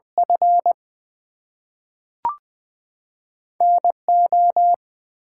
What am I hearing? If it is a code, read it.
Morse code, heard as a keyed tone, about 20 wpm, 700 Hz. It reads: F N O
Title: Two Letter Words - 20wpm